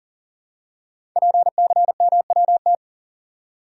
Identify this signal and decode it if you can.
Morse code, heard as a keyed tone, about 42 wpm, 690 Hz. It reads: PCMWT
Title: Code Groups - 5 Letters - 40wpm